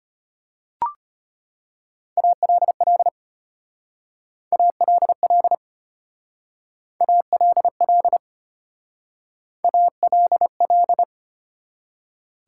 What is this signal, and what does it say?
Morse code, heard as a keyed tone, about 32 wpm, 700 Hz. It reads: ALL ALL ALL ALL